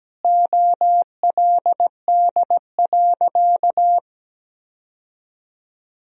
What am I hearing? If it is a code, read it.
Morse code, heard as a keyed tone, about 17 wpm, 700 Hz. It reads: OLD.
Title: Sentences from Top 1000 Words - Review Part 1 - 17wpm